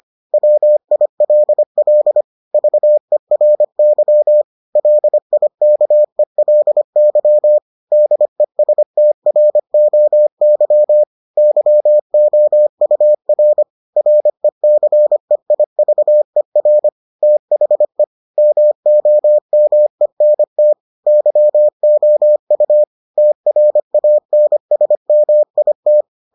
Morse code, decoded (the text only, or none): WILL VERY LIKELY DESTROY YOUR RECEIVER THE MOMENT YOU TRANSMIT